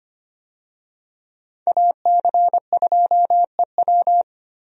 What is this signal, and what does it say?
Morse code, heard as a keyed tone, about 25 wpm, 700 Hz. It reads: AC2EW
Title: USA Call Signs - 5 Characters - Repeated 3x - 25wpm